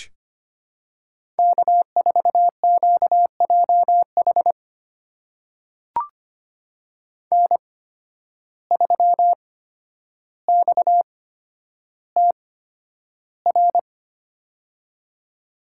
Morse code, heard as a keyed tone, about 25 wpm, spaced as for 11 wpm, 700 Hz. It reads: K4QJH N3XTR
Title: USA Call Signs - 5 Characters - 8x Character Spacing - 25wpm